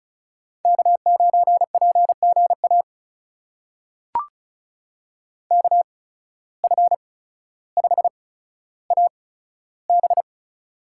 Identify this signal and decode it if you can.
Morse code, heard as a keyed tone, about 34 wpm, spaced as for 15 wpm, 700 Hz. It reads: K9PGA KF5AB
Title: USA Call Signs - 5 Characters - 8x Character Spacing - 35wpm